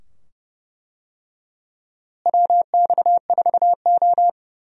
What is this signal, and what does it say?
Morse code, heard as a keyed tone, about 30 wpm, 700 Hz. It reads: WX4O
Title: USA Call Signs - 4 Characters - Repeated 2x - 30wpm